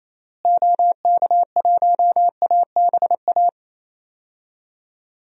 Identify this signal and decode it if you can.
Morse code, heard as a keyed tone, about 28 wpm, 700 Hz. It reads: OK1ABA
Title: DXCC-100 Call Signs - 6 Characters - Repeated 3x - 28wpm